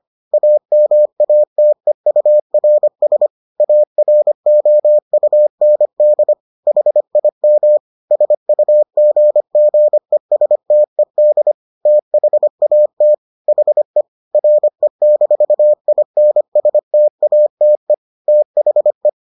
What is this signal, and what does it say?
Morse code, heard as a keyed tone, about 25 wpm, 600 Hz. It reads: AMATEURS AROUND HIM SUGGESTED THAT HE RE-INSTATE THE